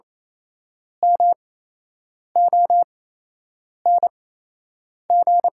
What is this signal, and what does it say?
Morse code, heard as a keyed tone, about 28 wpm, spaced as for 8 wpm, 700 Hz. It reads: MONG
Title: 8x Character Spacing - Top 500 Words - 28wpm